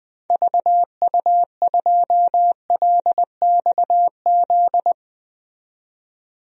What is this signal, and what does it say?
Morse code, heard as a keyed tone, about 20 wpm, 700 Hz. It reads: VU2LXZ
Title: DXCC-100 Call Signs - 6 Characters - 20wpm